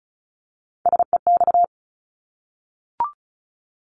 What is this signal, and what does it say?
Morse code, heard as a keyed tone, about 36 wpm, 700 Hz. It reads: SEX
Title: Multi-letter Phonemes Repeated 3x 35wpm